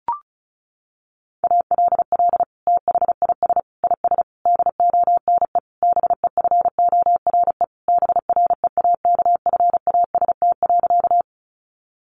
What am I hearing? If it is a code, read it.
Morse code, heard as a keyed tone, about 34 wpm, 700 Hz. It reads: ALL THIS IS DONE BEFORE BREAKFAST.
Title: Sentences from Top 1500 Words 35wpm